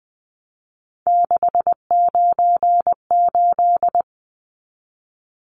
Morse code, heard as a keyed tone, about 20 wpm, 700 Hz. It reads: 698
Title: Numbers - 3-Digit 20wpm